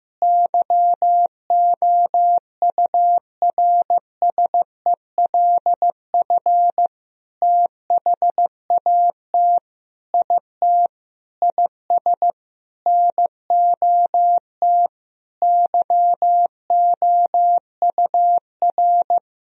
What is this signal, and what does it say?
Morse code, heard as a keyed tone, about 15 wpm, 700 Hz. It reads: YOURSELF THAT IT IS NOT YOUR
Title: Sentences from Top 1000 Words - Review Part 1 - 15wpm